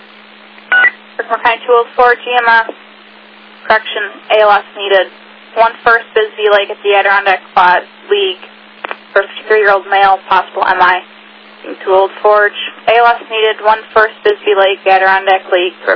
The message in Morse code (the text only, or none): none